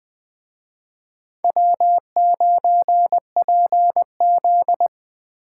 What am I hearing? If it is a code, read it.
Morse code, heard as a keyed tone, about 20 wpm, 700 Hz. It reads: W9PZ
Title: Speed Racing - USA 1x2 Call Signs 17wpm